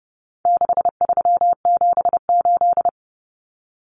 Morse code, heard as a keyed tone, about 30 wpm, 700 Hz. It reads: 6378
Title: Numbers - 4-Digit 30wpm